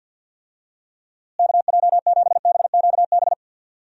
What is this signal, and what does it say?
Morse code, heard as a keyed tone, about 50 wpm, 700 Hz. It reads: KJ7BQB
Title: USA Call Signs - 6 Characters - Repeated 3x - 50wpm